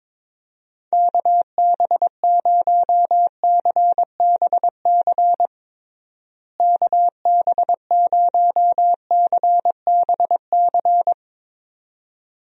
Morse code, heard as a keyed tone, about 22 wpm, 700 Hz. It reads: KB0CBC KB0CBC
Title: USA Call Signs - 6 Characters - Repeated 2x - 22wpm